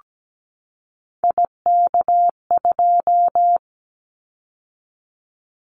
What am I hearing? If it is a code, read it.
Morse code, heard as a keyed tone, about 17 wpm, 700 Hz. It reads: IK2
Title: Callsign Prefixes for Top 25 DXCC - 17wpm